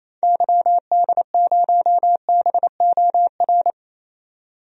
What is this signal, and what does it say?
Morse code, heard as a keyed tone, about 28 wpm, 700 Hz. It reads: YD0BOR
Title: DXCC-100 Call Signs - 6 Characters - 28wpm